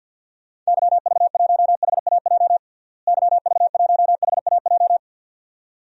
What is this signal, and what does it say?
Morse code, heard as a keyed tone, about 50 wpm, 700 Hz. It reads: YV1HAJ YV1HAJ
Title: DXCC-100 Call Signs - 6 Characters - Repeated 2x - 50wpm